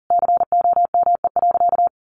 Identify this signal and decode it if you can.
Morse code, heard as a keyed tone, about 40 wpm, 700 Hz. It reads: COME.
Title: Sentences from Top 1500 Words 40wpm